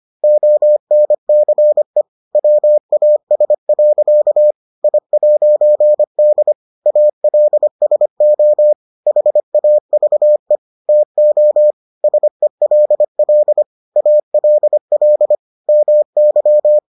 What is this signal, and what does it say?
Morse code, heard as a keyed tone, about 25 wpm, 600 Hz. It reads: ONCE WAS. I'D ALSO HAVE TO SELL ALL MY